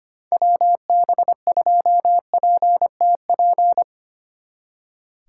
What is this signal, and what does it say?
Morse code, heard as a keyed tone, about 32 wpm, 700 Hz. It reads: WB2PTP